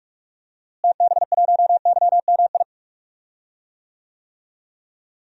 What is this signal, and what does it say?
Morse code, heard as a keyed tone, about 44 wpm, 700 Hz. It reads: TB1YMI
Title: DXCC-100 Call Signs - 6 Characters - Repeated 2x - 45wpm